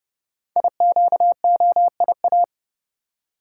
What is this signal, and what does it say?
Morse code, heard as a keyed tone, about 30 wpm, 700 Hz. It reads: IQOIA
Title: Code Groups - 5 Letters - 30wpm